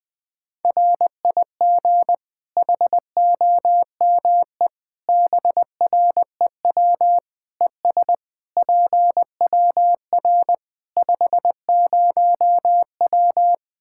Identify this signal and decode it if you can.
Morse code, heard as a keyed tone, about 21 wpm, 700 Hz. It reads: RIG HOME BREW ES PWR 50W
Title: Practice QSO - Exchange 2 - Info - 20wpm